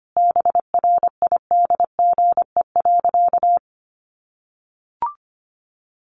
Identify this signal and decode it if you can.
Morse code, heard as a keyed tone, about 25 wpm, 700 Hz. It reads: BRIDGE.
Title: Sentences from Top 1200 Words 25wpm